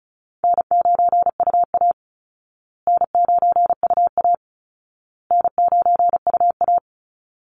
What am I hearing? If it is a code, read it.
Morse code, heard as a keyed tone, about 34 wpm, 700 Hz. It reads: N9UA N9UA N9UA
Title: Easy Call Signs Repeated 3x - 35wpm